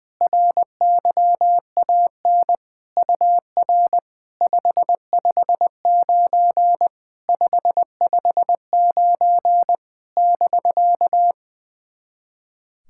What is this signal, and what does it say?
Morse code, heard as a keyed tone, about 20 wpm, 690 Hz. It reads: RYAN UR 559 559 <BK>
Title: SOTA QSO Practice 20wpm